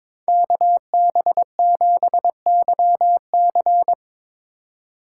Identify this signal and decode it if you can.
Morse code, heard as a keyed tone, about 24 wpm, 700 Hz. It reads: KB7YC
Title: Speed Racing - USA 2x2 Call Signs 22wpm